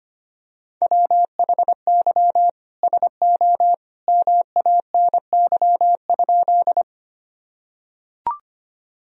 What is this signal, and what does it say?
Morse code, heard as a keyed tone, about 25 wpm, 700 Hz. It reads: WHY SO MANY?
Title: Sentences from Top 1000 Words - Review Part 1 - 25wpm